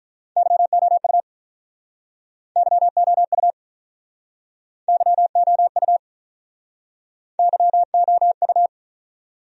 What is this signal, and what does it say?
Morse code, heard as a keyed tone, about 48 wpm, 700 Hz. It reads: YOU YOU YOU YOU